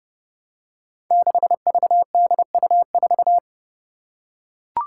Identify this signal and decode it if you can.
Morse code, heard as a keyed tone, about 30 wpm, 700 Hz. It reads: 6VDU4
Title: Five Challenging Letter-Number Combinations 30wpm